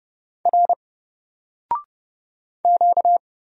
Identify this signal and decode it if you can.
Morse code, heard as a keyed tone, about 30 wpm, 700 Hz. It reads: R Q
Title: Single Character (Letter:Number:Punctuation:Prosign) Alt 30wpm